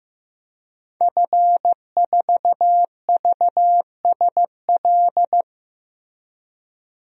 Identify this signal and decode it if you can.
Morse code, heard as a keyed tone, about 15 wpm, 700 Hz. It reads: F4VSL
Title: DXCC-100 Call Signs - 5 Characters - Repeated 2x - 15wpm